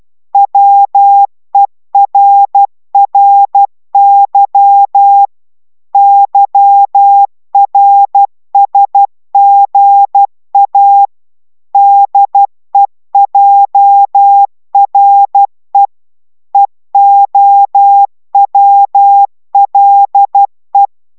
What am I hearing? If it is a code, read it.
Morse code, heard as a keyed tone, about 12 wpm, 800 Hz. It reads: WERRY YRSGA DEJRE EOWLE